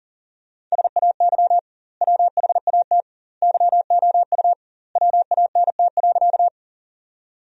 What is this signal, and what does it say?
Morse code, heard as a keyed tone, about 40 wpm, 700 Hz. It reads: SAY WHAT YOU WANT.
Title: 2x Word-Spacing - Sentences from Top 200 Words - 40wpm